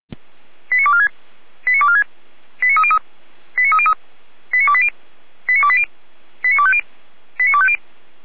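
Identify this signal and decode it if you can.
Five-tone selective call (ZVEI), heard as quick runs of five tones, tones of about 70 ms. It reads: 89246 89246 89303 89303 89280 89280 89250 89250
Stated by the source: ZVEI five-tone call received by SDR, from Feuerwehr Orlamünde audio server.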